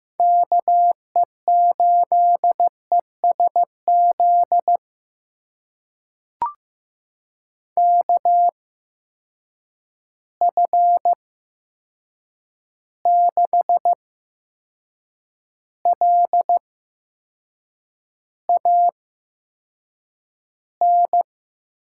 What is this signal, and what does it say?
Morse code, heard as a keyed tone, about 15 wpm, spaced as for 6 wpm, 700 Hz. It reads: KE8ESZ KF6LAN
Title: USA Call Signs - 6 Characters - 8x Character Spacing - 15wpm